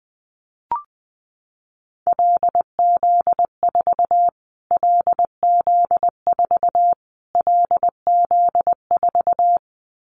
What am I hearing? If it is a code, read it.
Morse code, heard as a keyed tone, about 20 wpm, 700 Hz. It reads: LZ4 LZ4 LZ4
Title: Callsign Prefixes for Top 25 DXCC - Repeated 3x - 20wpm